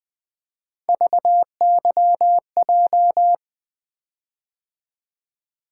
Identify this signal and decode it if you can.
Morse code, heard as a keyed tone, about 20 wpm, 700 Hz. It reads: VYJ
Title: Three Letter-Number Combinations - 20wpm